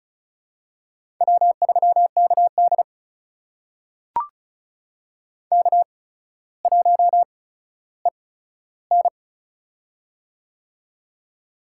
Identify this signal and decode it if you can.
Morse code, heard as a keyed tone, about 36 wpm, spaced as for 15 wpm, 700 Hz. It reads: W3KD K1EN
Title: USA Call Signs - 4 Characters - 8x Character Spacing - 35wpm